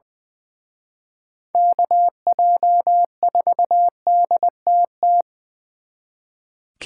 Morse code, heard as a keyed tone, about 24 wpm, 700 Hz. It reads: KJ4DTT
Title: Speed Racing - USA Call Signs 20wpm